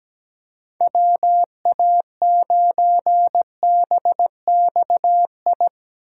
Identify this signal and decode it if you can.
Morse code, heard as a keyed tone, about 17 wpm, 700 Hz. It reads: WA9BXI
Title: USA Call Signs - 6 Characters - 2x Character Spacing - 17wpm